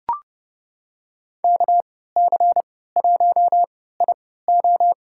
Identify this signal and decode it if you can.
Morse code, heard as a keyed tone, about 32 wpm, spaced as for 17 wpm, 700 Hz. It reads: KC1IO
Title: USA Call Signs - 3x Character Spacing - 30wpm